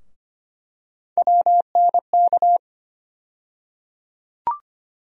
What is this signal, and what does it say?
Morse code, heard as a keyed tone, about 25 wpm, 700 Hz. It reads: WNK
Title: USA Call Sign Postfixes - 3 Letters 25wpm